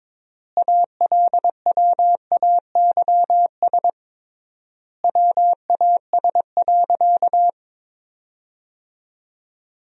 Morse code, heard as a keyed tone, about 22 wpm, 700 Hz. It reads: ALWAYS WAS.
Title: 3x Word-Spacing - Sentences from Top 300 Words - 22wpm